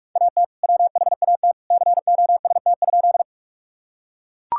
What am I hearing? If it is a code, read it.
Morse code, heard as a keyed tone, about 42 wpm, 690 Hz. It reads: AT WHAT COST?
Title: Sentences from Top 1000 Words - Review Part 5 - 45wpm